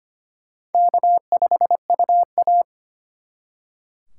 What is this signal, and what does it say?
Morse code, heard as a keyed tone, about 32 wpm, 700 Hz. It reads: K5UA